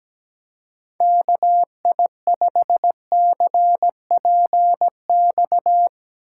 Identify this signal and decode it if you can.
Morse code, heard as a keyed tone, about 17 wpm, 700 Hz. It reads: KI5CPX
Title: Speed Racing - USA Call Signs 17wpm